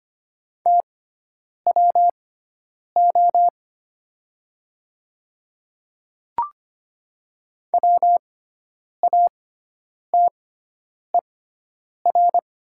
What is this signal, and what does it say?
Morse code, heard as a keyed tone, about 25 wpm, spaced as for 9 wpm, 700 Hz. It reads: TWO WATER
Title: 6x Character Spacing - Top 200 Words - 25wpm